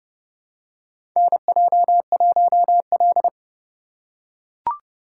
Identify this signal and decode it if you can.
Morse code, heard as a keyed tone, about 31 wpm, 700 Hz. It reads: NJ1L
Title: USA Call Signs - 4 Characters - 6x Character Spacing - 30wpm